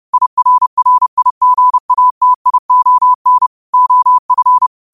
Morse code, heard as a keyed tone, about 30 wpm, 1000 Hz. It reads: IRRIGATION OF
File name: qotdfiles / 30wpm